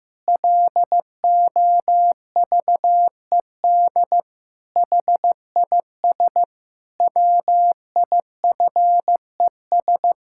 Morse code, heard as a keyed tone, about 15 wpm, 700 Hz. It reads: LOVED HIS WIFES
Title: Pangram Challenge v2 - 15wpm